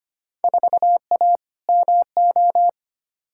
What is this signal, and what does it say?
Morse code, heard as a keyed tone, about 25 wpm, 700 Hz. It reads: 4A MO
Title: Field Day Exchanges - 25wpm